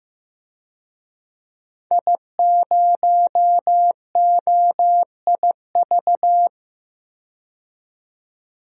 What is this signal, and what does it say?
Morse code, heard as a keyed tone, about 15 wpm, 700 Hz. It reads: I0OIV
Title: DXCC-100 Call Signs - 5 Characters - 15wpm